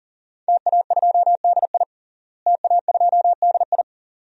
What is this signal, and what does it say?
Morse code, heard as a keyed tone, about 40 wpm, 700 Hz. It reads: TA2DI TA2DI
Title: DXCC-100 Call Signs - 5 Characters - Repeated 2x - 40wpm